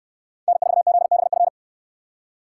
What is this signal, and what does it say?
Morse code, heard as a keyed tone, about 67 wpm, 700 Hz. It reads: N4ZBF